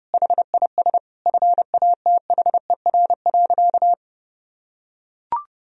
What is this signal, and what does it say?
Morse code, heard as a keyed tone, about 32 wpm, 700 Hz. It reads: HIS FATHER.
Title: Sentences from Top 1000 Words - Review Part 1 - 30wpm